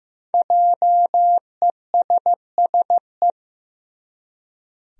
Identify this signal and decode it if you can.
Morse code, heard as a keyed tone, about 15 wpm, 700 Hz. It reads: JESSE
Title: USA Common Names - 15wpm